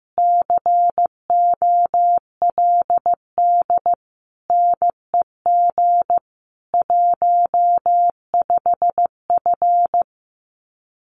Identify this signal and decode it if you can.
Morse code, heard as a keyed tone, about 15 wpm, 700 Hz. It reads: COLD NEG 15F
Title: Common QSO Phrases Encore 15wpm